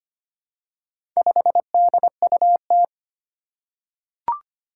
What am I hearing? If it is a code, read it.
Morse code, heard as a keyed tone, about 25 wpm, 700 Hz. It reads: 5DUT
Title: Four Challenging Letter-Number Combinations 25wpm